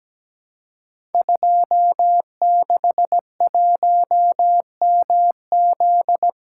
Morse code, heard as a keyed tone, about 17 wpm, 700 Hz. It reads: 261MZ
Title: Code Groups - 5 Letter:Numbers - 17wpm